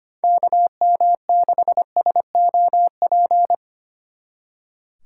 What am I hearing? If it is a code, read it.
Morse code, heard as a keyed tone, about 32 wpm, 700 Hz. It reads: KM6SOP